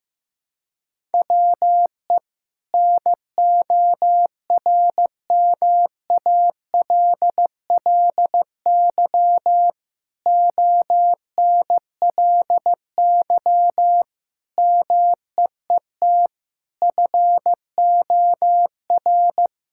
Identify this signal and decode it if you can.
Morse code, heard as a keyed tone, about 15 wpm, 700 Hz. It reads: WE NORMALLY ONLY MEET FOR
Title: Sentences from Top 1000 Words - Review Part 6 - 15wpm